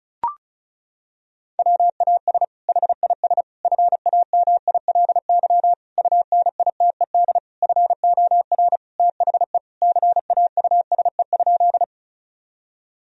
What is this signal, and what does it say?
Morse code, heard as a keyed tone, about 34 wpm, 700 Hz. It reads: WAS HIS FAMILY UNITED FOR THE CAUSE?
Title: Sentences from Top 1000 Words - Review Part 4 - 35wpm